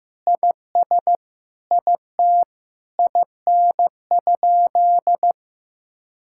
Morse code, heard as a keyed tone, about 16 wpm, 700 Hz. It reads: IS IT IN?
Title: Sentences from Top 1600 Words 15wpm